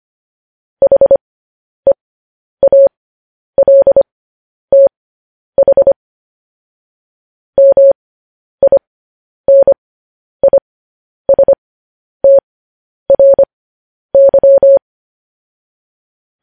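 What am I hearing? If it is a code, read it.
Morse code, heard as a keyed tone, about 25 wpm, spaced as for 10 wpm, 550 Hz. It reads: HEALTH MINISTRY
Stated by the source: Morse code transmission of news headlines.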